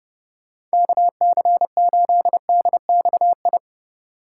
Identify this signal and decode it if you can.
Morse code, heard as a keyed tone, about 33 wpm, 700 Hz. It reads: KC8DXI